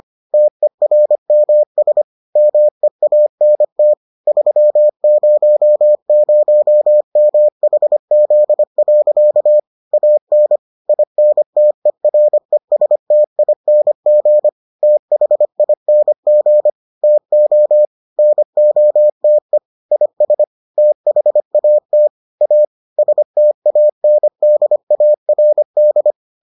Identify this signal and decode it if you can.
Morse code, heard as a keyed tone, about 25 wpm, 600 Hz. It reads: TERMS MEANT 300MHZ. AN INTERESTING THING TO NOTE IS THAT A STANDARD